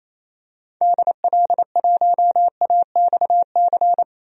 Morse code, heard as a keyed tone, about 28 wpm, 700 Hz. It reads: DL1AXC
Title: DXCC-100 Call Signs - 6 Characters - 28wpm